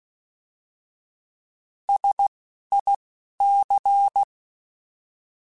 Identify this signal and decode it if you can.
Morse code, heard as a keyed tone, about 16 wpm, spaced as for 11 wpm, 800 Hz. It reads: SIC
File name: Alphabetical / Alphabetical mp3s / Ss @10